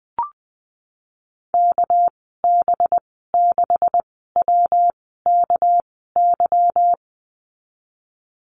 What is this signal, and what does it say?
Morse code, heard as a keyed tone, about 19 wpm, spaced as for 14 wpm, 700 Hz. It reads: KB6WKY
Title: 207 Lesson 33g - Letter j - Call Signs 20wpm